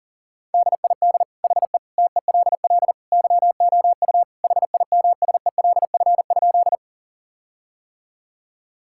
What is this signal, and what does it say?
Morse code, heard as a keyed tone, about 40 wpm, 700 Hz. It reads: DID HE TELL YOU HIMSELF?
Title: Sentences from Top 1600 Words 40wpm